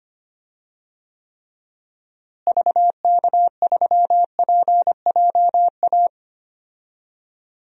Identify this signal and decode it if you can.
Morse code, heard as a keyed tone, about 25 wpm, 700 Hz. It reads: VK3PJA
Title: DXCC-100 Call Signs - 6 Characters - Repeated 2x - 25wpm